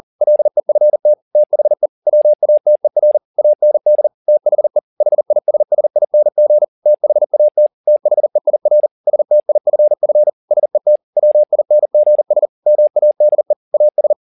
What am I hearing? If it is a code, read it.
Morse code, heard as a keyed tone, about 40 wpm, 600 Hz. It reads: LEFT THE WATER AND THE HISSING THAT THEIR STIFF SET WINGS MADE AS